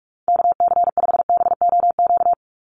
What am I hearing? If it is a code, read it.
Morse code, heard as a keyed tone, about 44 wpm, 700 Hz. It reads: KC5BOQ
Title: USA Call Signs 45wpm